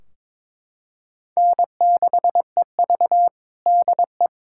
Morse code, heard as a keyed tone, about 22 wpm, 700 Hz. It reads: N6EV DE
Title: Practice QSO - Exchange 4 - Ending - 22wpm